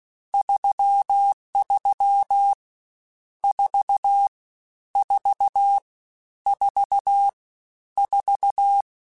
Morse code, heard as a keyed tone, about 16 wpm, 800 Hz. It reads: 33 4 4 4 4
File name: Misc / Misc-mp3s / Numb @15